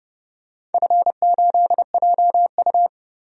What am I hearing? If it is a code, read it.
Morse code, heard as a keyed tone, about 32 wpm, 700 Hz. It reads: F8JU
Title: DXCC-100 Call Signs - 4 Characters - Repeated 2x - 30wpm